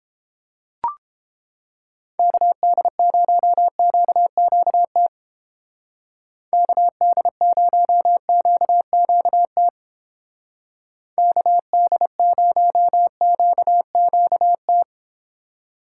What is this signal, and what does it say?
Morse code, heard as a keyed tone, about 29 wpm, 700 Hz. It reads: KD0QQT KD0QQT KD0QQT